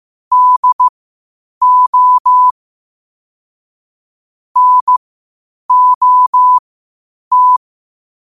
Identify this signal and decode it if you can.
Morse code, heard as a keyed tone, about 15 wpm, spaced as for 8 wpm, 1000 Hz. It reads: DO NOT
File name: qotdfiles / 8wpm